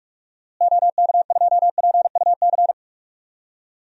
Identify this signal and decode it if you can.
Morse code, heard as a keyed tone, about 44 wpm, 700 Hz. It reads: OK2PUC
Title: DXCC-100 Call Signs - 6 Characters - Repeated 3x - 45wpm